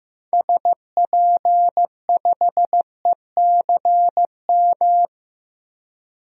Morse code, heard as a keyed tone, about 15 wpm, 700 Hz. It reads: SP5ECM
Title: DXCC-100 Call Signs - 6 Characters - 15wpm